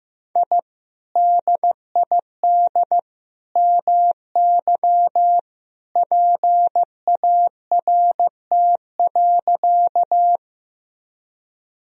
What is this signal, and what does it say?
Morse code, heard as a keyed tone, about 15 wpm, 700 Hz. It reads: I DID MY PART.
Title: Sentences from Top 1000 Words - Review Part 1 - 15wpm